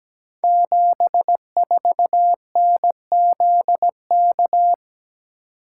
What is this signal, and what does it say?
Morse code, heard as a keyed tone, about 17 wpm, 700 Hz. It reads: 74NZK
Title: Code Groups - 5 Letter:Numbers - 17wpm